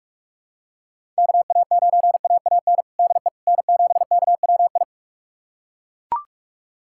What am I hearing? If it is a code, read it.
Morse code, heard as a keyed tone, about 41 wpm, 700 Hz. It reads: KA9AAN DE N7KWI
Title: USA Callsign DE Callsign 45wpm